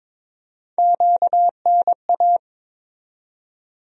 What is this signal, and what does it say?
Morse code, heard as a keyed tone, about 22 wpm, 700 Hz. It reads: QNA
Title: USA Call Sign Postfixes - 3 Letters 22wpm